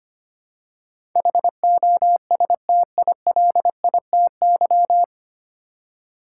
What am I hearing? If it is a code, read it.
Morse code, heard as a keyed tone, about 25 wpm, 700 Hz. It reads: HOSTILITY
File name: practice / 25wpm / 9 Letter Words - First 3 Letters Given - 25wpm